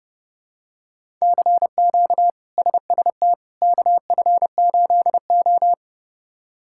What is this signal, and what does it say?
Morse code, heard as a keyed tone, about 30 wpm, 700 Hz. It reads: CQ SST KF8O
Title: K1USN SST QSO Practice - 30wpm